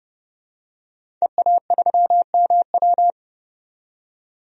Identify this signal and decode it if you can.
Morse code, heard as a keyed tone, about 30 wpm, 700 Hz. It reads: EA3MW
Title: DXCC-100 Call Signs - 5 Characters - 30wpm